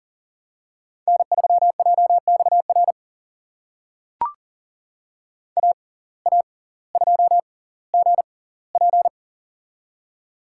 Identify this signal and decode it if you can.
Morse code, heard as a keyed tone, about 40 wpm, spaced as for 20 wpm, 700 Hz. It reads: N3JXR AA2GP
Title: USA Call Signs - 5 Characters - 6x Character Spacing - 40wpm